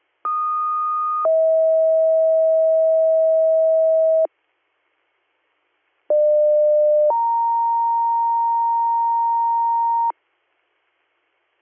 Two-tone page, heard as a one-second tone, then a three-second tone, about 1230 then 640 Hz, about 600 then 930 Hz. It is heard from 0.2 to 10.1 s.